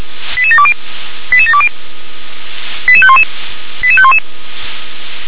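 Five-tone selective call (ZVEI), heard as quick runs of five tones, tones of about 70 ms. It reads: 88720 88720 88511 88511
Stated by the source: ZVEI five-tone call received on SDR, code 88631.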